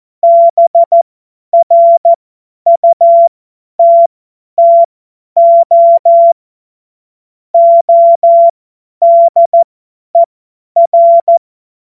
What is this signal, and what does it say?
Morse code, heard as a keyed tone, about 14 wpm, spaced as for 10 wpm, 680 Hz. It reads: BRUTTO ODER
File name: Morsekurs / Witze / Witz 19. April 2022 50 F70